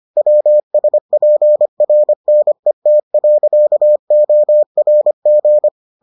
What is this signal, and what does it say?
Morse code, heard as a keyed tone, about 25 wpm, 600 Hz. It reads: WSPRNET.ORG